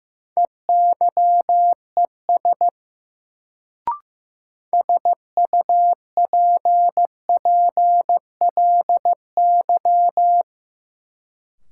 Morse code, heard as a keyed tone, about 16 wpm, 700 Hz. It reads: EYES SUPPLY
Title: Sentences from Top 1000 Words - Review Part 4 - New Word Drill - 15wpm